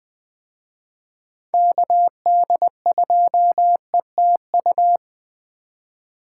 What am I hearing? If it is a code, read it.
Morse code, heard as a keyed tone, about 21 wpm, 700 Hz. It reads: KD2ETU